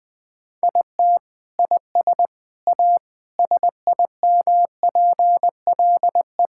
Morse code, heard as a keyed tone, about 20 wpm, 700 Hz. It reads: IT IS A SIMPLE
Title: Sentences from Top 1000 Words - Review Part 6 - 20wpm